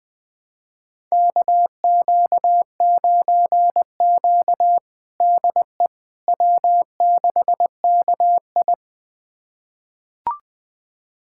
Morse code, heard as a keyed tone, about 20 wpm, 700 Hz. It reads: KQ9Q DE W6KI
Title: USA Callsign DE Callsign - 4 Characters - 20wpm